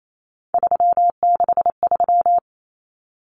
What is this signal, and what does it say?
Morse code, heard as a keyed tone, about 28 wpm, 700 Hz. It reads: 363
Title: Common Cut Numbers - 3-Digit 28wpm